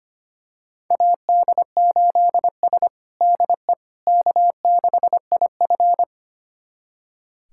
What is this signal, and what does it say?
Morse code, heard as a keyed tone, about 25 wpm, 700 Hz. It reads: AD8S DE K6IF
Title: USA Callsign DE Callsign - 4 Characters - 25wpm